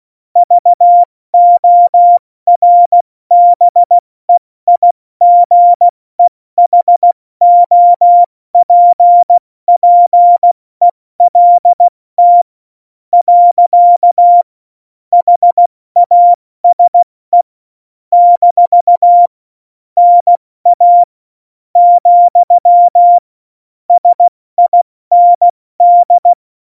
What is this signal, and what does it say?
Morse code, heard as a keyed tone, about 16 wpm, 700 Hz. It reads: VORBEIGEHOPPELT . HASE - NA , SIND
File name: Morsekurs / Witze / Witz 28. August 2018, 70 F80